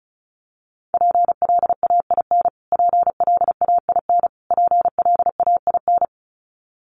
Morse code, heard as a keyed tone, about 34 wpm, 700 Hz. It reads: PLAIN PLAIN PLAIN
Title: Top 500 Words Repeated 35wpm